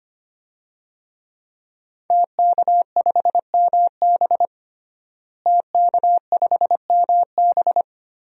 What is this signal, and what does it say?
Morse code, heard as a keyed tone, about 25 wpm, 700 Hz. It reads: TK5MB TK5MB
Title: DXCC-100 Call Signs - 5 Characters - Repeated 3x - 25wpm